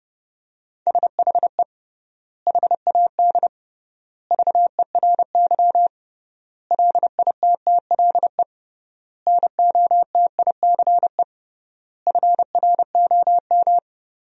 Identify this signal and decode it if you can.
Morse code, heard as a keyed tone, about 30 wpm, 700 Hz. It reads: SHE HAD VERY LITTLE NOTICE FROM
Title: 3x Word-Spacing - Sentences from Top 400 Words - 30wpm